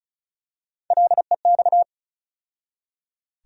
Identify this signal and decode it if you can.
Morse code, heard as a keyed tone, about 36 wpm, 700 Hz. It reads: LEX
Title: USA Call Sign Postfixes - 3 Letters 35wpm